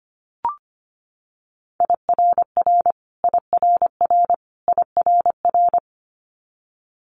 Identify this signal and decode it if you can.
Morse code, heard as a keyed tone, about 25 wpm, 700 Hz. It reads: IRR IRR IRR
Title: Multi-letter Phonemes Repeated 3x 25wpm